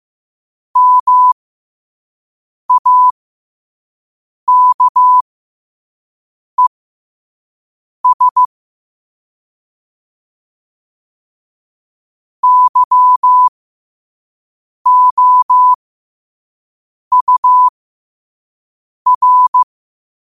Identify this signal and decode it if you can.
Morse code, heard as a keyed tone, about 15 wpm, spaced as for 5 wpm, 1000 Hz. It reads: MAKES YOUR
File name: qotdfiles / 5wpm